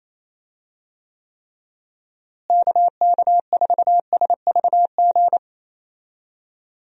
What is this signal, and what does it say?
Morse code, heard as a keyed tone, about 28 wpm, 700 Hz. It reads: KK4SVG
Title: USA Call Signs - 6 Characters - Repeated 3x - 28wpm